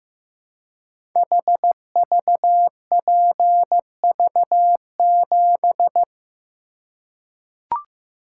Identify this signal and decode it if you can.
Morse code, heard as a keyed tone, about 15 wpm, 700 Hz. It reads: HVPV7
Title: Code Groups - 5 Letter:Numbers - 15wpm